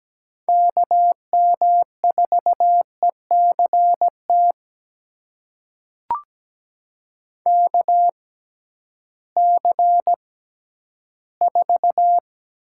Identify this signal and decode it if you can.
Morse code, heard as a keyed tone, about 17 wpm, spaced as for 11 wpm, 700 Hz. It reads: KM4ECT KC4
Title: USA Call Signs - 6 Characters - 6x Character Spacing - 17wpm